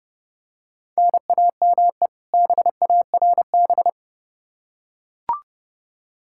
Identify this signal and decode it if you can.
Morse code, heard as a keyed tone, about 32 wpm, 700 Hz. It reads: NAME BARB
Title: K1USN SST QSO Practice 30wpm